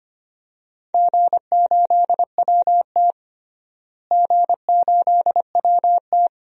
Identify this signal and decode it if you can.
Morse code, heard as a keyed tone, about 25 wpm, 700 Hz. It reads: G8WT G8WT
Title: DXCC-100 Call Signs - 4 Characters - Repeated 3x - 25wpm